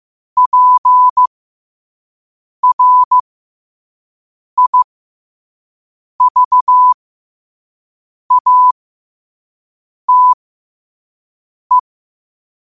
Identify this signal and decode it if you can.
Morse code, heard as a keyed tone, about 15 wpm, spaced as for 5 wpm, 1000 Hz. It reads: PRIVATE
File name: qotdfiles / 5wpm